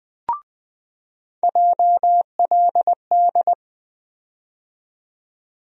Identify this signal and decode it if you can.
Morse code, heard as a keyed tone, about 20 wpm, 700 Hz. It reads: JLD
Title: USA Call Sign Postfixes - 3 Letters 20wpm